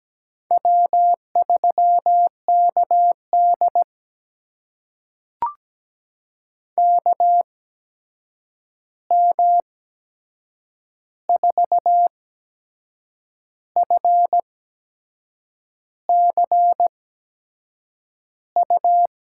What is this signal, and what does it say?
Morse code, heard as a keyed tone, about 17 wpm, spaced as for 6 wpm, 700 Hz. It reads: W3KD KM4FCU
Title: USA Call Signs - 8x Character Spacing - 17wpm